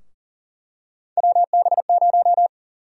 Morse code, heard as a keyed tone, about 40 wpm, 700 Hz. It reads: WB0
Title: USA Call Sign Prefixes - Repeated 2x - 40wpm